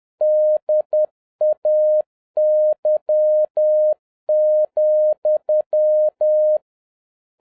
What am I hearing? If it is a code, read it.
Morse code, heard as a keyed tone, about 10 wpm, 610 Hz. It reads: DAY,